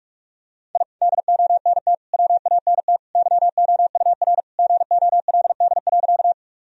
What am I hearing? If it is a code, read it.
Morse code, heard as a keyed tone, about 45 wpm, 700 Hz. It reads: I DONT WANT YOUR GOLD.
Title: Sentences from Top 1000 Words - Review Part 3 - 45wpm